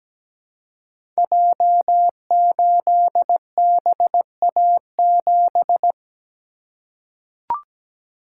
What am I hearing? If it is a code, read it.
Morse code, heard as a keyed tone, about 17 wpm, 700 Hz. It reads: J8BA7
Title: Code Groups - 5 Letter:Numbers - 17wpm